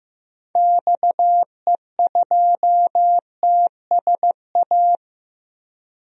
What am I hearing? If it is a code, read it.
Morse code, heard as a keyed tone, about 15 wpm, 700 Hz. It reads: XE2TSA
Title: DXCC-100 Call Signs - 6 Characters - Repeated 2x - 15wpm